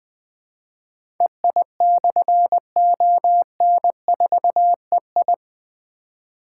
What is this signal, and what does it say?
Morse code, heard as a keyed tone, about 20 wpm, 700 Hz. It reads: EI/ON4EI
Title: Difficult Call Signs 20wpm